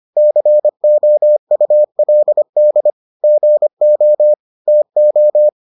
Morse code, heard as a keyed tone, about 25 wpm, 600 Hz. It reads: COULD GO TO